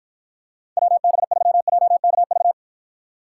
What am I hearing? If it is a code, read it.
Morse code, heard as a keyed tone, about 52 wpm, 700 Hz. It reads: WB3JKV